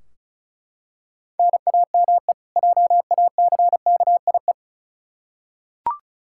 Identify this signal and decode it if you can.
Morse code, heard as a keyed tone, about 34 wpm, 700 Hz. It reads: NAME JACKIE
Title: Practice QSO - Exchange 1 - Beginning - 35wpm